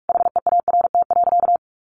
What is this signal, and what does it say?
Morse code, heard as a keyed tone, about 44 wpm, 700 Hz. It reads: HEART.
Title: Sentences from Top 1200 Words 45wpm